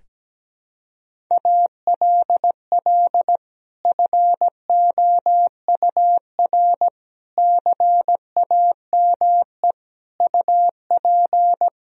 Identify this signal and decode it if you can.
Morse code, heard as a keyed tone, about 17 wpm, 700 Hz. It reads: ALL FOUR CAME UP
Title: Sentences from Top 1000 Words - Review Part 1 - 17wpm